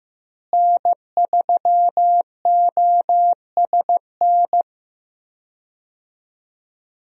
Morse code, heard as a keyed tone, about 15 wpm, 700 Hz. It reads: N3OSN
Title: USA Call Signs - 5 Characters - Repeated 2x - 15wpm